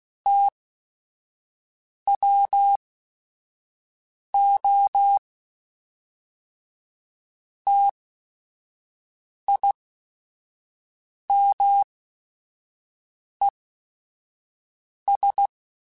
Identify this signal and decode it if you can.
Morse code, heard as a keyed tone, about 16 wpm, spaced as for 5 wpm, 800 Hz. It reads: TWO TIMES
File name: Technical Subjects / Technical Subjects-mp3s / XL @5